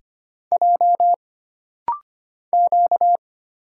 Morse code, heard as a keyed tone, about 25 wpm, 700 Hz. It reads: J Q
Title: Single Letters - Rapid-Fire 25wpm